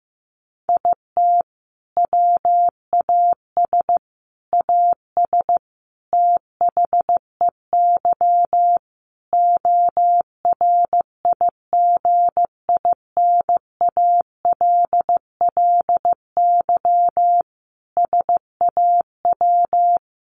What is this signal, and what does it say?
Morse code, heard as a keyed tone, about 15 wpm, 700 Hz. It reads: IT WAS AS THEY ORIGINALLY SAW
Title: Sentences from Top 1500 Words 15wpm